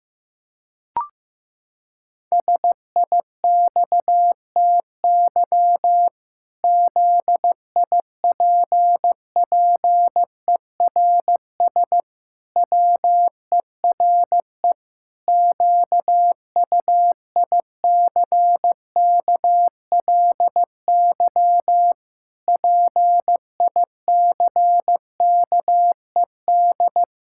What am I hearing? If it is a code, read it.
Morse code, heard as a keyed tone, about 15 wpm, 700 Hz. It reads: SIXTY ZIPPERS WERE QUICKLY PICKED
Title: Pangram Challenge v2 - 15wpm